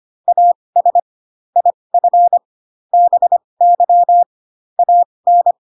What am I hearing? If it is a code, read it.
Morse code, heard as a keyed tone, about 25 wpm, spaced as for 20 wpm, 700 Hz. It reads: AS IF BY AN
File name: Pinocchio / Pinocchio ch8 15 25 ik1hge